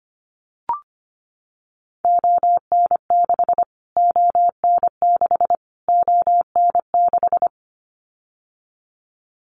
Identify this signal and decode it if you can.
Morse code, heard as a keyed tone, about 25 wpm, 700 Hz. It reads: ON6 ON6 ON6
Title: Callsign Prefixes for Top 25 DXCC - Repeated 3x - 25wpm